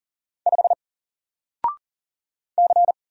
Morse code, heard as a keyed tone, about 39 wpm, 700 Hz. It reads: 5 C
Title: Single Character (Letter:Number:Punctuation:Prosign) 40wpm